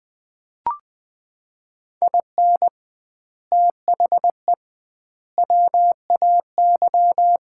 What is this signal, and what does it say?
Morse code, heard as a keyed tone, about 20 wpm, 700 Hz. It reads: IN THE WAY
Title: 2x Word-Spacing - Sets of 3 Words - 20wpm